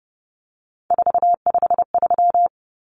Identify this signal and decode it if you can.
Morse code, heard as a keyed tone, about 30 wpm, 700 Hz. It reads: 453
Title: Common Cut Numbers - 3-Digit 30wpm